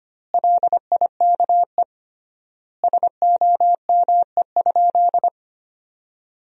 Morse code, heard as a keyed tone, about 25 wpm, 700 Hz. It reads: LIKE SOME?
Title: 3x Word-Spacing - Sentences from Top 100 Words - 25wpm